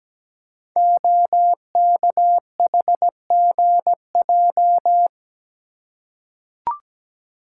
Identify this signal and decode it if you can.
Morse code, heard as a keyed tone, about 17 wpm, 700 Hz. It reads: OKHGJ
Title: Code Groups - 5 Letter:Numbers - 17wpm